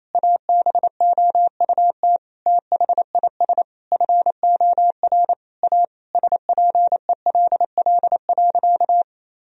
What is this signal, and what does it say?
Morse code, heard as a keyed tone, about 28 wpm, 700 Hz. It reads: ABOUT THIS FOR A SPELL.
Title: Sentences from Top 1000 Words - Review Part 1 - 28wpm